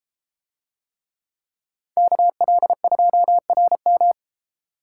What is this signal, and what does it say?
Morse code, heard as a keyed tone, about 32 wpm, 700 Hz. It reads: KL2RM